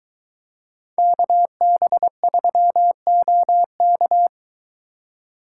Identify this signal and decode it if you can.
Morse code, heard as a keyed tone, about 23 wpm, 700 Hz. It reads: KB3OK